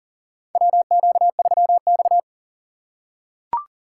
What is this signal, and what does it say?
Morse code, heard as a keyed tone, about 37 wpm, 700 Hz. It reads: WQ3X
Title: USA Call Signs - 4 Characters - 3x Character Spacing - 40wpm